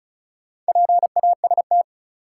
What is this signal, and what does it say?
Morse code, heard as a keyed tone, about 36 wpm, 700 Hz. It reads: PAST
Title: Sentences from Top 600 Words - New Word Drill - 35wpm